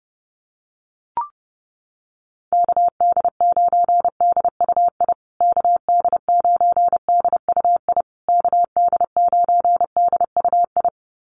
Morse code, heard as a keyed tone, about 28 wpm, 700 Hz. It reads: KD9DUI KD9DUI KD9DUI
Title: USA Call Signs Repeated 3x 30wpm